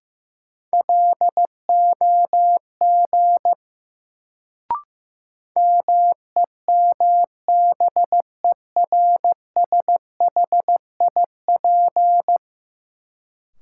Morse code, heard as a keyed tone, about 15 wpm, 700 Hz. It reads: LOG MEMBERSHIP
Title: Sentences from Top 1400 Words - New Word Drill Alt - 15wpm